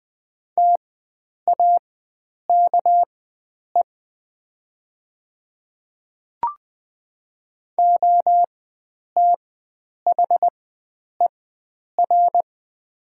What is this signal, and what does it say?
Morse code, heard as a keyed tone, about 20 wpm, spaced as for 9 wpm, 700 Hz. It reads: TAKE OTHER
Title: 4x Character Spacing - Top 300 Words - 20wpm